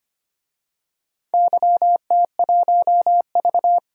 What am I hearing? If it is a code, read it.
Morse code, heard as a keyed tone, about 25 wpm, 700 Hz. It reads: YT1V
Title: DXCC-100 Call Signs - 4 Characters - 25wpm